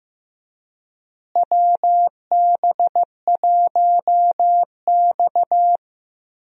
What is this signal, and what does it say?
Morse code, heard as a keyed tone, about 15 wpm, 700 Hz. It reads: WB1X
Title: USA Call Signs - 4 Characters - Repeated 2x - 15wpm